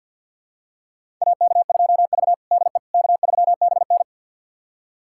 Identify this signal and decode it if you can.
Morse code, heard as a keyed tone, about 46 wpm, 700 Hz. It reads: AK2V DE K3BN
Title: USA Callsign DE Callsign - 4 Characters - 50wpm